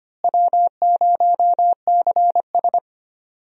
Morse code, heard as a keyed tone, about 25 wpm, 700 Hz. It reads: W0CS
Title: USA Call Signs - 4 Characters - Repeated 2x - 25wpm